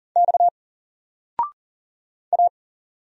Morse code, heard as a keyed tone, about 40 wpm, 700 Hz. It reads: X A
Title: Single Letter-Number 40wpm